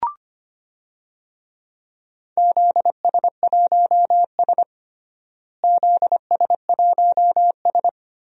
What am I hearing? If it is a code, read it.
Morse code, heard as a keyed tone, about 25 wpm, 700 Hz. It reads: ZS1S ZS1S
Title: DXCC-100 Call Signs - 4 Characters - Repeated 2x - 25wpm